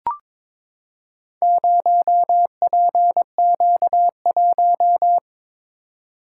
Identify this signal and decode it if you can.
Morse code, heard as a keyed tone, about 22 wpm, 700 Hz. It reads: 0PQ1
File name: practice / 22wpm / Four Letter-Number Combinations - 22wpm